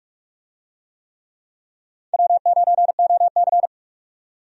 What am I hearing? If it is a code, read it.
Morse code, heard as a keyed tone, about 45 wpm, 700 Hz. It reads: W9OC
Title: USA Call Signs - 4 Characters - 45wpm